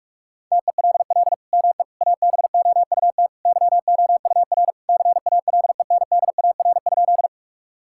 Morse code, heard as a keyed tone, about 44 wpm, 700 Hz. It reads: TELL ME ABOUT YOUR CALENDAR?
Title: Sentences from Top 1600 Words 45wpm